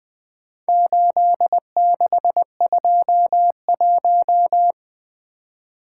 Morse code, heard as a keyed tone, about 22 wpm, 700 Hz. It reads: 8621
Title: Speed-Racing - Four Letter-Number Combinations - 20wpm